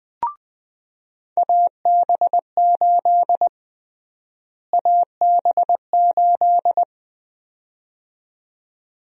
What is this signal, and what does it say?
Morse code, heard as a keyed tone, about 20 wpm, 700 Hz. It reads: AB8 AB8
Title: USA Call Sign Prefixes - Repeated 2x - 20wpm